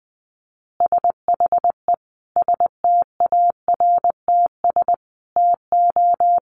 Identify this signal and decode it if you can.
Morse code, heard as a keyed tone, about 20 wpm, 700 Hz. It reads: SHE STARTS TO
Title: Easy Phrases 20wpm